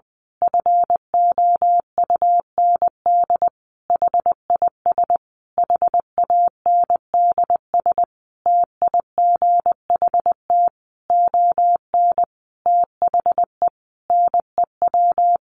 Morse code, heard as a keyed tone, about 20 wpm, 700 Hz. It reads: FOUND HIS HANDS TIGHT ON THE NEW
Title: A Sound Of Thunder 20wpm